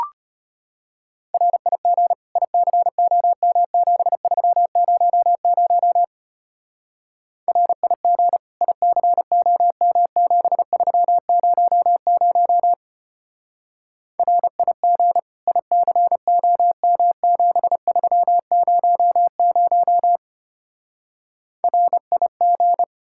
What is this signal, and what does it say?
Morse code, heard as a keyed tone, about 31 wpm, 700 Hz. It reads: RIG ICOM7300 RIG ICOM7300 RIG ICOM7300 RIG